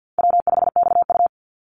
Morse code, heard as a keyed tone, about 50 wpm, 700 Hz. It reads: W5XU
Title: Call Signs - 50wpm